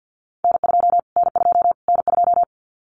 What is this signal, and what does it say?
Morse code, heard as a keyed tone, about 50 wpm, 700 Hz. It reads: N2 N2 N2
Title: Callsign Prefixes for Top 25 DXCC - Repeated 3x - 50wpm